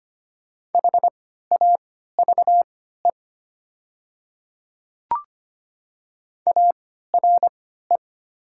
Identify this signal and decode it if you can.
Morse code, heard as a keyed tone, about 25 wpm, spaced as for 14 wpm, 700 Hz. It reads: HAVE ARE
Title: 3x Character Spacing - Top 200 Words - 25wpm